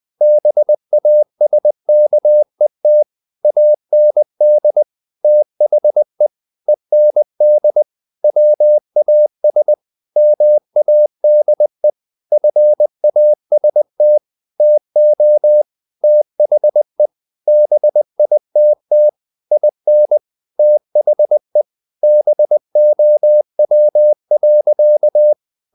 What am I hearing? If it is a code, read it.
Morse code, heard as a keyed tone, about 20 wpm, 600 Hz. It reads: BASKET AND THE END WAS MADE FAST TO THE BITT IN THE BOW.
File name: the old man and the sea part 2 20wpm